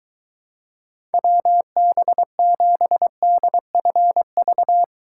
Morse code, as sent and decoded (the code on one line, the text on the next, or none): .-- -... --... -.. ..-. ...-
WB7DFV